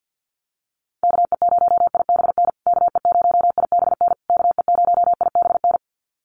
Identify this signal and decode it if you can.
Morse code, heard as a keyed tone, about 50 wpm, 700 Hz. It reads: KE0IBN KE0IBN KE0IBN
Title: USA Call Signs Repeated 3x 50wpm